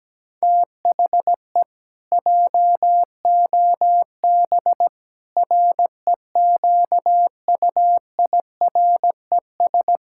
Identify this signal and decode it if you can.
Morse code, heard as a keyed tone, about 17 wpm, 700 Hz. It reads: THE JOB REQUIRES
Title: Pangram Challenge v2 - 17wpm